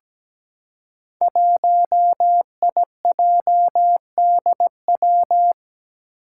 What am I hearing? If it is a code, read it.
Morse code, heard as a keyed tone, about 17 wpm, 700 Hz. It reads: 1IJDW
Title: Five Challenging Letter-Number Combinations 17wpm